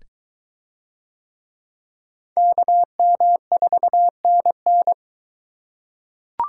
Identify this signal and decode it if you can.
Morse code, heard as a keyed tone, about 23 wpm, 700 Hz. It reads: KM4NN